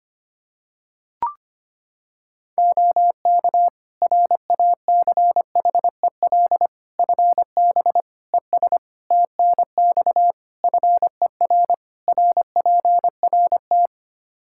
Morse code, heard as a keyed tone, about 25 wpm, 700 Hz. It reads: OK RACHEL FB ES TNX FER RPRT
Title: Practice QSO - Exchange 2 - Info - 25wpm